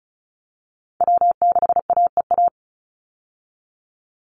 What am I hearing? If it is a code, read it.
Morse code, heard as a keyed tone, about 34 wpm, 700 Hz. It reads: W6AEA
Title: Call Signs - 35wpm